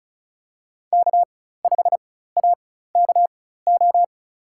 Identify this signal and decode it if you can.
Morse code, heard as a keyed tone, about 34 wpm, spaced as for 16 wpm, 700 Hz. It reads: K5AKO
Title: USA Call Signs - 5 Characters - 4x Character Spacing - 35wpm